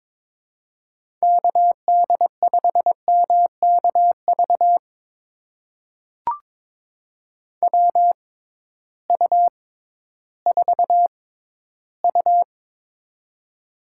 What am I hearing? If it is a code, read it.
Morse code, heard as a keyed tone, about 22 wpm, spaced as for 13 wpm, 700 Hz. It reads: KD5MKV WU4U
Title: USA Call Signs - 6x Character Spacing - 22wpm